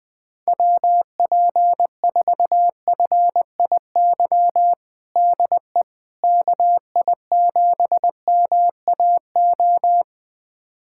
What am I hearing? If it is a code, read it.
Morse code, heard as a keyed tone, about 20 wpm, 700 Hz. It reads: WP4FIY DE KI7MAO
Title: USA Callsign DE Callsign 20wpm